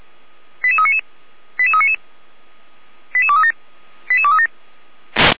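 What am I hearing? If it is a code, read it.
Five-tone selective call (ZVEI), heard as quick runs of five tones, tones of about 70 ms. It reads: 88399 88399 88237 88237
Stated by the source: ZVEI five-tone call received by SDR, from the Feuerwehr Orlamünde (fire brigade) audio server.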